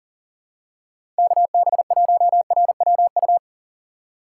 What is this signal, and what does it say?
Morse code, heard as a keyed tone, about 43 wpm, 700 Hz. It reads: KB1RWU
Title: USA Call Signs - 6 Characters - 6x Character Spacing - 40wpm